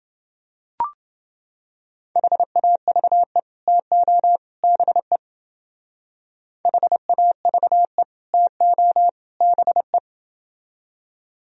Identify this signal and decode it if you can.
Morse code, heard as a keyed tone, about 28 wpm, 700 Hz. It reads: HAVE TO BE HAVE TO BE